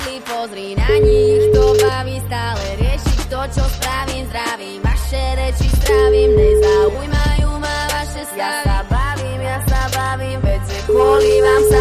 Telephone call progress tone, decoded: ringback tone

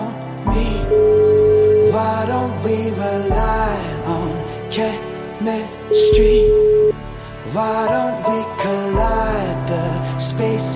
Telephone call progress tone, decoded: ringback tone

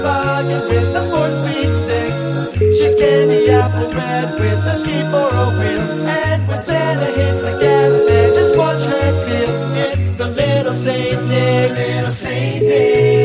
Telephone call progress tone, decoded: ringback tone

